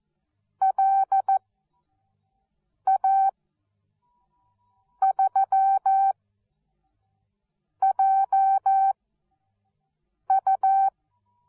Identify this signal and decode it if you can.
Morse code, heard as a keyed tone, about 14 wpm, spaced as for 5 wpm, 780 Hz. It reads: LA3JU
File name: Onlinelehrgang / morsen / DJ4UF Morsen13